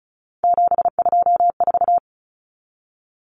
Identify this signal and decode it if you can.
Morse code, heard as a keyed tone, about 34 wpm, 700 Hz. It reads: 724
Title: Numbers - 3-Digit 35wpm